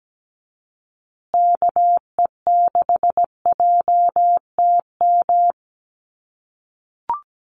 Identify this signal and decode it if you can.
Morse code, heard as a keyed tone, about 17 wpm, 700 Hz. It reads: KE6JTM
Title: USA Call Signs Repeated 3x 17wpm